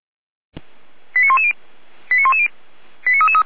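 Five-tone selective call (ZVEI), heard as quick runs of five tones, tones of about 70 ms. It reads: 89110 89110 89333
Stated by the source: ZVEI five-tone selective call received on SDR.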